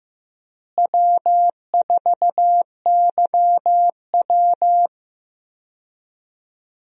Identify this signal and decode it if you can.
Morse code, heard as a keyed tone, about 15 wpm, 700 Hz. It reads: W4YW
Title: USA Call Signs - 4 Characters - Repeated 2x - 15wpm